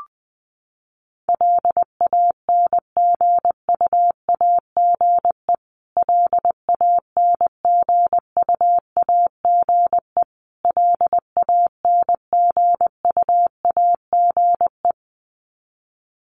Morse code, heard as a keyed tone, about 20 wpm, 700 Hz. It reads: LANGUAGE LANGUAGE LANGUAGE
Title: Top 500 Words Repeated 20wpm